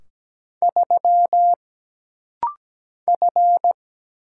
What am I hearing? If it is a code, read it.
Morse code, heard as a keyed tone, about 17 wpm, 700 Hz. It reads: 3 F
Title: Single Letter-Number Alt - Rapid-Fire 17wpm